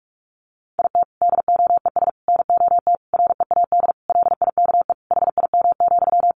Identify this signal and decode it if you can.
Morse code, heard as a keyed tone, about 44 wpm, 700 Hz. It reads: IT DOES NOT READ LIKE HIM,
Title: Sentences from Top 800 Words 45wpm